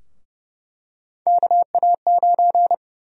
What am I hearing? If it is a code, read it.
Morse code, heard as a keyed tone, about 30 wpm, 700 Hz. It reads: KA9
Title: USA Call Sign Prefixes - 30wpm